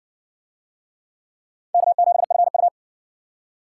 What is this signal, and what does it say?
Morse code, heard as a keyed tone, about 55 wpm, 700 Hz. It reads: K7FU